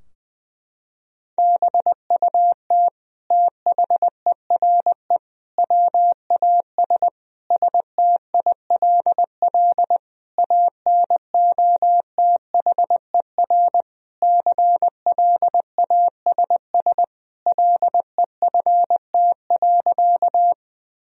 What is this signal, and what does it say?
Morse code, heard as a keyed tone, about 21 wpm, 700 Hz. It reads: BUT THERE WAS STILL ANOTHER CLASS LEFT.
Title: Sentences from Top 1000 Words - Review Part 2 - 20wpm